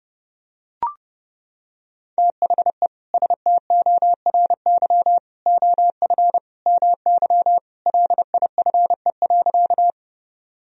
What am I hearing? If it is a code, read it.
Morse code, heard as a keyed tone, about 30 wpm, 700 Hz. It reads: THE STORY OF MY LIFE.
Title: Sentences from Top 1000 Words - Review Part 1 - 30wpm